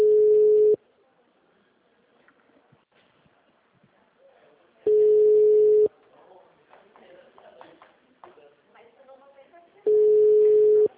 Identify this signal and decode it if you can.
Ringback tone (Europe, 425 Hz), with speech.